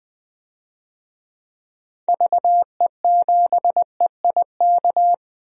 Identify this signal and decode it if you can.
Morse code, heard as a keyed tone, about 20 wpm, 700 Hz. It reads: VE7EIK
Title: DXCC-100 Call Signs - 6 Characters - Repeated 3x - 20wpm